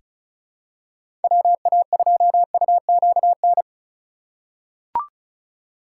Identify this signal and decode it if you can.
Morse code, heard as a keyed tone, about 34 wpm, 700 Hz. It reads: WA2UQN
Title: USA Call Signs - 6 Characters - 2x Character Spacing - 35wpm